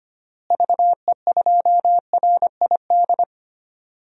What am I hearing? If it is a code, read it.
Morse code, heard as a keyed tone, about 25 wpm, 700 Hz. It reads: VE2RID
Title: DXCC-100 Call Signs - 6 Characters - Repeated 3x - 25wpm